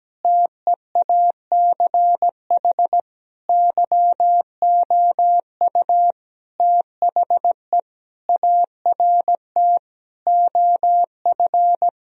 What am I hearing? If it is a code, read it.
Morse code, heard as a keyed tone, about 17 wpm, 700 Hz. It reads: TEACH YOU THE ART OF
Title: Sentences from Top 1000 Words - Review Part 3 - 17wpm